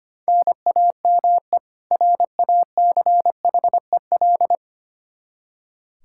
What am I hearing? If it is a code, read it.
Morse code, heard as a keyed tone, about 25 wpm, 700 Hz. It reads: NAME RACHEL
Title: Practice QSO - Exchange 1 - Beginning - 25wpm